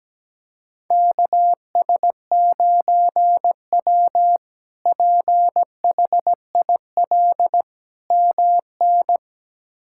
Morse code, heard as a keyed tone, about 17 wpm, 700 Hz. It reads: KS9W PHIL MN
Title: K1USN SST QSO Practice - 17wpm